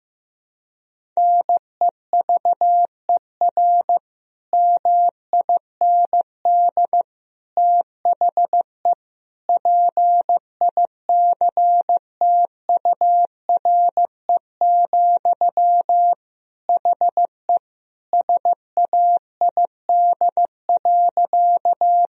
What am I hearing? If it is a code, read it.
Morse code, heard as a keyed tone, about 15 wpm, 700 Hz. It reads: NEVER MIND THE PICTURE, HE SAID.
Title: Sentences from Top 1000 Words - Review Part 1 - 15wpm